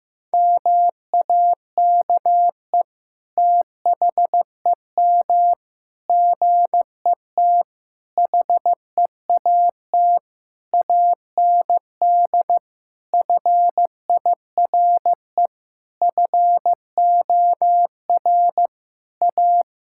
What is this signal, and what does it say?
Morse code, heard as a keyed tone, about 16 wpm, 700 Hz. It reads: MAKE THEM GET HEAT AND FIRE FOR A
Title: Sentences from Top 1600 Words 15wpm